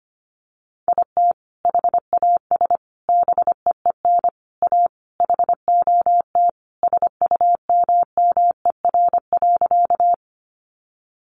Morse code, heard as a keyed tone, about 25 wpm, 700 Hz. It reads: IT HAS BEEN A HOT SUMMER.
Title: Sentences from Top 1100 Words 25wpm